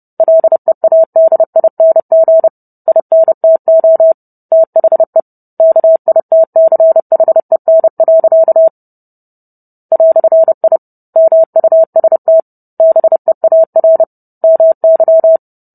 Morse code, heard as a keyed tone, about 30 wpm, 650 Hz. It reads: LEADING INTO THE KITCHEN. "I MUST BEAR MY